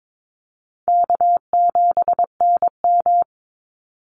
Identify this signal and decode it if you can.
Morse code, heard as a keyed tone, about 22 wpm, 700 Hz. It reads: K7NM
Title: Easy Call Signs - 22wpm